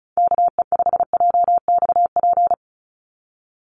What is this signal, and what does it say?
Morse code, heard as a keyed tone, about 34 wpm, 700 Hz. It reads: KE5JXP
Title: USA Call Signs 35wpm